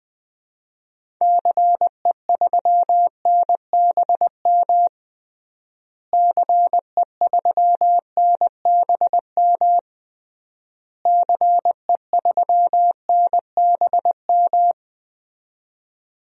Morse code, heard as a keyed tone, about 20 wpm, 700 Hz. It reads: CE3NBM CE3NBM CE3NBM
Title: DXCC-100 Call Signs - 6 Characters - Repeated 3x - 20wpm